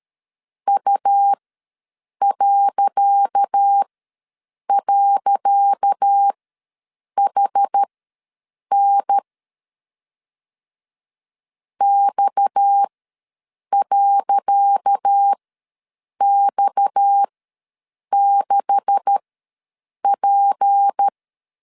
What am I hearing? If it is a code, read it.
Morse code, heard as a keyed tone, about 13 wpm, spaced as for 7 wpm, 800 Hz. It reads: U..HN X.X6P
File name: LEZIONI / 10 SIGNS / 10C DOT ws 12wpm 3548